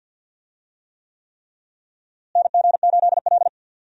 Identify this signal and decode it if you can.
Morse code, heard as a keyed tone, about 49 wpm, 700 Hz. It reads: NG8L
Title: USA Call Signs - 4 Characters - Repeated 3x - 50wpm